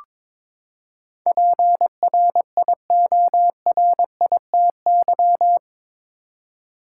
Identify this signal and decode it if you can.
Morse code, heard as a keyed tone, about 22 wpm, 700 Hz. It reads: PRIORITY
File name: practice / 22wpm / Eight Letter Words - 22wpm